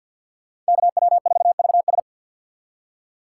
Morse code, heard as a keyed tone, about 64 wpm, 700 Hz. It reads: KW4VS